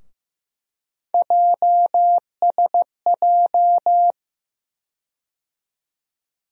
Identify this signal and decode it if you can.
Morse code, heard as a keyed tone, about 15 wpm, 700 Hz. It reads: JSJ
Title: USA Call Sign Postfixes - 3 Letters 15wpm